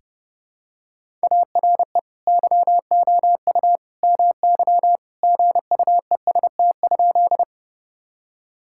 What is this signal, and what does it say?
Morse code, heard as a keyed tone, about 30 wpm, 700 Hz. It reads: ARE YOU MY GUEST?
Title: Sentences from Top 1600 Words 30wpm